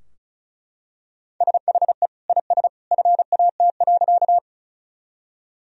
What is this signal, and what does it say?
Morse code, heard as a keyed tone, about 36 wpm, 700 Hz. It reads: SHE IS FAT.
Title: Sentences from Top 1000 Words - Review Part 4 - 35wpm